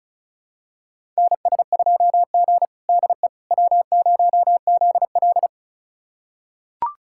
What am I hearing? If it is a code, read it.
Morse code, heard as a keyed tone, about 36 wpm, 700 Hz. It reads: NS2G DE W0ZL
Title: USA Callsign DE Callsign - 4 Characters - 35wpm